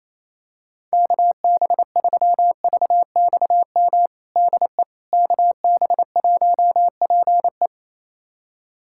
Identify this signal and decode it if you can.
Morse code, heard as a keyed tone, about 27 wpm, 700 Hz. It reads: KB3VXM DE KB1PE
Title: USA Callsign DE Callsign 28wpm